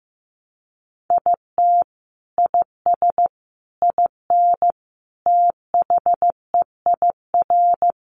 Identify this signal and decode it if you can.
Morse code, heard as a keyed tone, about 15 wpm, 700 Hz. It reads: IT IS IN THEIR
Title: Sentences from Top 1300 Words 15wpm